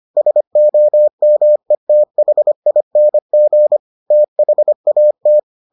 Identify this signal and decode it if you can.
Morse code, heard as a keyed tone, about 25 wpm, 600 Hz. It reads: SOMETHING THAT